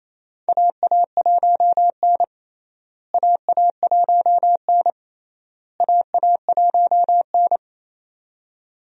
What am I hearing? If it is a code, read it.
Morse code, heard as a keyed tone, about 27 wpm, 700 Hz. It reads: AA1N AA1N AA1N
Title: USA Call Signs - 4 Characters - Repeated 3x - 28wpm